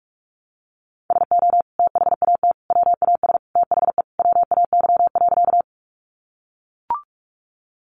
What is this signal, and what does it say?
Morse code, heard as a keyed tone, about 44 wpm, 700 Hz. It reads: SO THAT WAS THE WAY.
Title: Sentences from Top 100 Words 45wpm